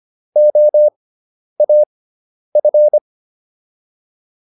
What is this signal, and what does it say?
Morse code, heard as a keyed tone, about 25 wpm, spaced as for 10 wpm, 600 Hz. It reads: OAF